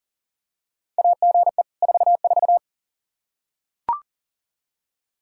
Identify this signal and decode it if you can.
Morse code, heard as a keyed tone, about 40 wpm, 700 Hz. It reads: AGE 44
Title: Practice QSO - Exchange 3 - More Info - 40wpm